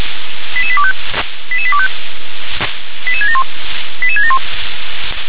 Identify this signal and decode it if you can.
Five-tone selective call (ZVEI), heard as quick runs of five tones, tones of about 70 ms. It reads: 88825 88825 88671 88671
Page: SDR/ZVEI